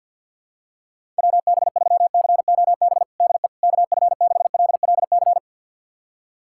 Morse code, heard as a keyed tone, about 51 wpm, 690 Hz. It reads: WB3COB DE KF6LLC
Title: USA Callsign DE Callsign - 6 Characters - 50wpm